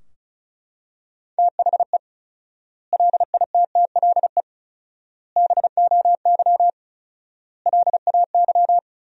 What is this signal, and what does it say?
Morse code, heard as a keyed tone, about 36 wpm, 700 Hz. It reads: THE LITTLE BOY LAY
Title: 4x Word-Spacing - Sentences from Top 500 Words - 35wpm